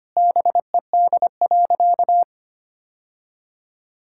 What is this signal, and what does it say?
Morse code, heard as a keyed tone, about 25 wpm, 700 Hz. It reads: BED.
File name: practice / 25wpm / Sentences with 3-Word Phrases - 25wpm